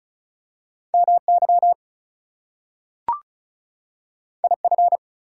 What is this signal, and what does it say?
Morse code, heard as a keyed tone, about 36 wpm, 700 Hz. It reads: MY IF
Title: Two Letter Words - 35wpm